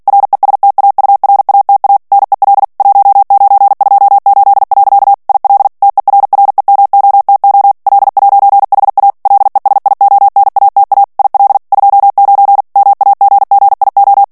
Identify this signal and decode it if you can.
Morse code, heard as a keyed tone, about 48 wpm, 800 Hz. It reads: RESTAURATA NEL 1928. IL TERREMOTO L'HA LESIONATA IL 29 MAGGIO
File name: morse mp3 / ITALIEN texte clair 1 vitesse 45 mots minute